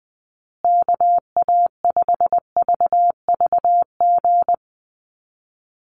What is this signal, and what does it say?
Morse code, heard as a keyed tone, about 20 wpm, 700 Hz. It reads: KA5VVG
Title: USA Call Signs Repeated 3x 20wpm